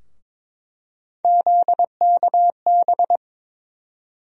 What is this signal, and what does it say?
Morse code, heard as a keyed tone, about 21 wpm, 700 Hz. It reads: ZKB
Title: USA Call Sign Postfixes - 3 Letters 22wpm